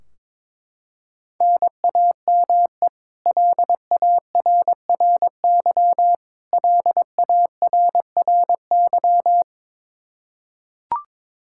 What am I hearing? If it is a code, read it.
Morse code, heard as a keyed tone, about 22 wpm, 700 Hz. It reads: NAME LARRY LARRY
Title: Practice QSO - Exchange 1 - Beginning - 22wpm